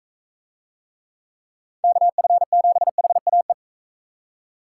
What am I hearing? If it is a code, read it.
Morse code, heard as a keyed tone, about 39 wpm, 700 Hz. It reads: KF7HAE